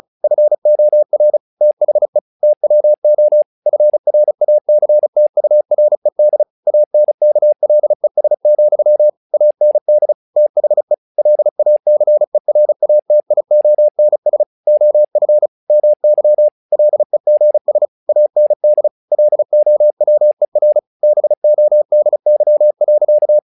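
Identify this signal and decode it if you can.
Morse code, heard as a keyed tone, about 35 wpm, 600 Hz. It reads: FOR THE TWO FRACTURED ANKLES, AND THE LACERATIONS OF MY LEGS AND LOWER BODY.
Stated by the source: Morse practice text "Tower Trouble" sent at 35 WPM, from the SKCC - Straight Key Century Club learning center audio files.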